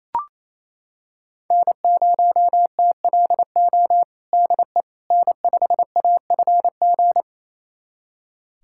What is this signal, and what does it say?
Morse code, heard as a keyed tone, about 27 wpm, 700 Hz. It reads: N0TLO DE N5AFG
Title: USA Callsign DE Callsign - 5 Characters - 28wpm